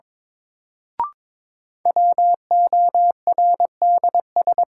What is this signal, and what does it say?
Morse code, heard as a keyed tone, about 22 wpm, 700 Hz. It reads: WORDS
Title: Sentences from Top 1000 Words - Review Part 6 - New Word Drill - 22wpm